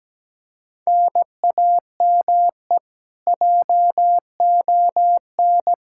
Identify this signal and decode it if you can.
Morse code, heard as a keyed tone, about 17 wpm, 700 Hz. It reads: NAME JON
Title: K1USN SST QSO Practice 17wpm